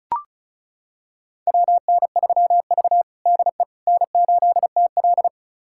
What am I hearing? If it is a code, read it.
Morse code, heard as a keyed tone, about 36 wpm, 700 Hz. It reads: WN3V DE N8TL
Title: USA Callsign DE Callsign - 4 Characters - 35wpm